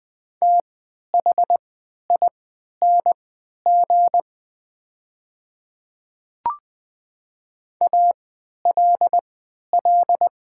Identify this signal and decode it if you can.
Morse code, heard as a keyed tone, about 20 wpm, spaced as for 11 wpm, 700 Hz. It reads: THING ALL
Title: 3x Character Spacing - Top 100 Words - 20wpm